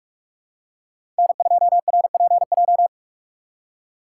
Morse code, heard as a keyed tone, about 46 wpm, 700 Hz. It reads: N2RPJ
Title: USA Call Signs - 5 Characters - 4x Character Spacing - 45wpm